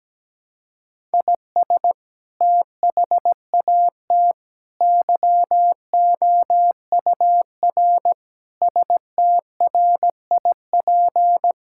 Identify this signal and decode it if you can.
Morse code, heard as a keyed tone, about 17 wpm, 700 Hz. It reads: IS THAT YOUR STRIP
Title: Sentences from Top 1600 Words 17wpm